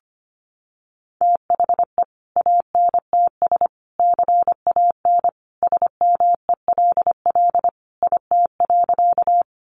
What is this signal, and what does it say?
Morse code, heard as a keyed tone, about 25 wpm, 700 Hz. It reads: THE ANTS CAN SMELL IT.
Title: Additional Easy Phrases 25wpm